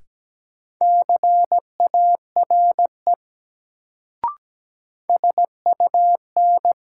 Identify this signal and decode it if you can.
Morse code, heard as a keyed tone, about 17 wpm, 700 Hz. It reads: CARE SUN
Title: Sentences from Top 300 Words - New Word Drill - 17wpm